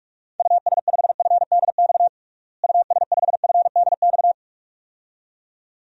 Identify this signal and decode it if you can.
Morse code, heard as a keyed tone, about 46 wpm, 700 Hz. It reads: US5FDX US5FDX
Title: DXCC-100 Call Signs - 6 Characters - Repeated 2x - 45wpm